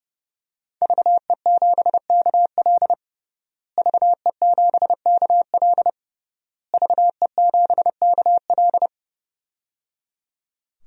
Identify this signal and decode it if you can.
Morse code, heard as a keyed tone, about 30 wpm, 700 Hz. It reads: VE7KL VE7KL VE7KL
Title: DXCC-100 Call Signs - 5 Characters - Repeated 3x - 30wpm